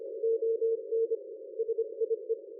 Morse code, heard as a keyed tone, about 25 wpm, 460 Hz. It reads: ON SIE
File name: dlazaawansowanych / nagraniazpasma / ExpertSDR2 line out 23.01.2020 14.12.00